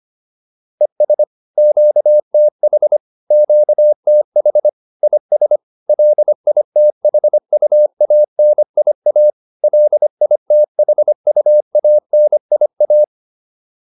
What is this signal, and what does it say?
Morse code, heard as a keyed tone, about 25 wpm, 600 Hz. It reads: ES QTH QTH IS LITHUANIA LITHUANIA
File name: QSO406 25